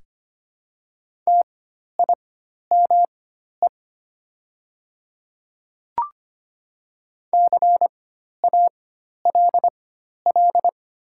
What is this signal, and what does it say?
Morse code, heard as a keyed tone, about 25 wpm, spaced as for 12 wpm, 700 Hz. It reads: TIME CALL
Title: 4x Character Spacing - Top 200 Words - 25wpm